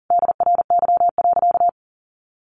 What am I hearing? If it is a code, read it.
Morse code, heard as a keyed tone, about 40 wpm, 700 Hz. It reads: DRY.
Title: Sentences from Top 1400 Words 40wpm